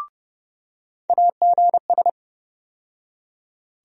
Morse code, heard as a keyed tone, about 32 wpm, 700 Hz. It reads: AGS
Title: USA Call Sign Postfixes - 3 Letters 30wpm